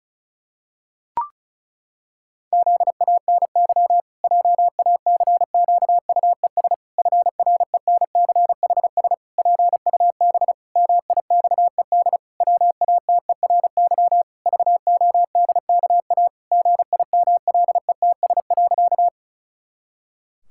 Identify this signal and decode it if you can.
Morse code, heard as a keyed tone, about 36 wpm, 700 Hz. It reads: ZANY JACQUES FRENCHS PUB MIXED WATERY VODKA GIMLETS.
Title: Pangram Challenge v2 - 35wpm